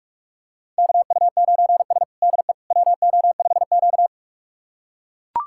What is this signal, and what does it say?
Morse code, heard as a keyed tone, about 41 wpm, 700 Hz. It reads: KU9S DE WO5Q
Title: USA Callsign DE Callsign - 4 Characters - 45wpm